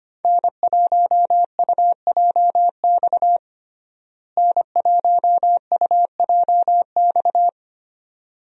Morse code, heard as a keyed tone, about 25 wpm, 700 Hz. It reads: N1UJX N1UJX
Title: USA Call Signs - 5 Characters - Repeated 2x - 25wpm